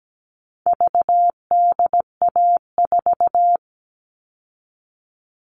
Morse code, heard as a keyed tone, about 17 wpm, 700 Hz. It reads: VDA4
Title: All Cut Numbers - 4-Digit 17wpm